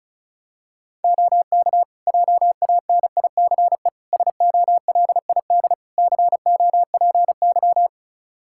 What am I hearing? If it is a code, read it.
Morse code, heard as a keyed tone, about 36 wpm, 700 Hz. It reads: OK JANICE SOLID COPY
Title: Practice QSO - Exchange 3 - More Info - 35wpm